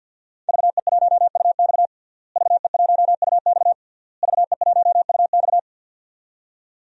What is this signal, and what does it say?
Morse code, heard as a keyed tone, about 50 wpm, 700 Hz. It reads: VE1UX VE1UX VE1UX
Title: DXCC-100 Call Signs - 5 Characters - Repeated 3x - 50wpm